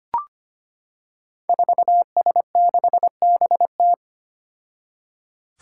Morse code, heard as a keyed tone, about 25 wpm, 700 Hz. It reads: 4S6BT
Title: Five Challenging Letter-Number Combinations 25wpm